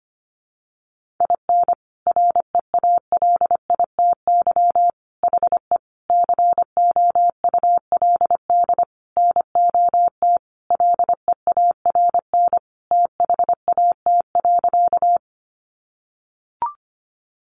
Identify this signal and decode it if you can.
Morse code, heard as a keyed tone, about 25 wpm, 700 Hz. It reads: IN REALITY HE COULD NOT LEARN THAT.
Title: Sentences from Top 1200 Words 25wpm